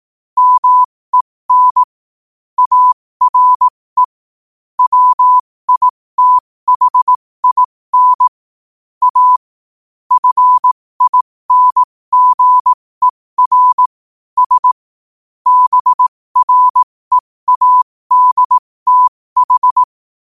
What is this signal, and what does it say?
Morse code, heard as a keyed tone, about 18 wpm, spaced as for 15 wpm, 1000 Hz. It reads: MEN ARE WITHIN A FINGER S BREADTH
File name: qotdfiles / 15wpm